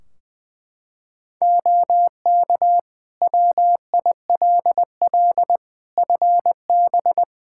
Morse code, heard as a keyed tone, about 20 wpm, 700 Hz. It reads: OK WILL FB
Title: Practice QSO - Exchange 2 - Info - 20wpm